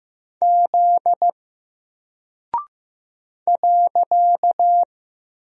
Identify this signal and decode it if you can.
Morse code, heard as a keyed tone, about 15 wpm, 700 Hz. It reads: Z .
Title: Single Character (Letter:Number:Punctuation:Prosign) 15wpm